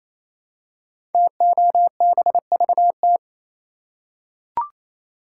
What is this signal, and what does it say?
Morse code, heard as a keyed tone, about 27 wpm, 700 Hz. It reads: TOBVT
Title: Code Groups - 5 Letters - 28wpm